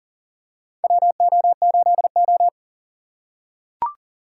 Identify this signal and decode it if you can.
Morse code, heard as a keyed tone, about 40 wpm, 700 Hz. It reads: WO8O
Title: USA Call Signs - 4 Characters - 3x Character Spacing - 40wpm